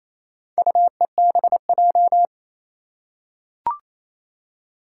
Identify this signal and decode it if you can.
Morse code, heard as a keyed tone, about 27 wpm, 700 Hz. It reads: UEBJ
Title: Four Challenging Letter-Number Combinations 28wpm